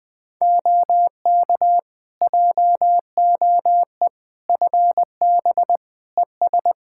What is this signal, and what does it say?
Morse code, heard as a keyed tone, about 20 wpm, 700 Hz. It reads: OK JOE FB ES
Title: K1USN SST QSO Practice 20wpm